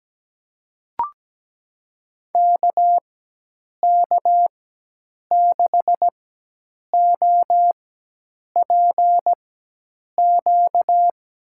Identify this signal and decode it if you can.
Morse code, heard as a keyed tone, about 17 wpm, spaced as for 8 wpm, 700 Hz. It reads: KK6OPQ
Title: USA Call Signs - 6 Characters - 4x Character Spacing - 17wpm